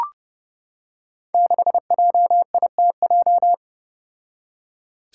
Morse code, heard as a keyed tone, about 32 wpm, 700 Hz. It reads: 6JITJ
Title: Five Challenging Letter-Number Combinations 30wpm